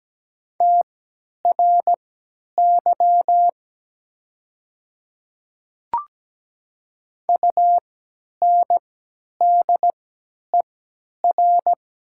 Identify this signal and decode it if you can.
Morse code, heard as a keyed tone, about 17 wpm, spaced as for 10 wpm, 690 Hz. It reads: TRY UNDER
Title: 3x Character Spacing - Top 200 Words - 17wpm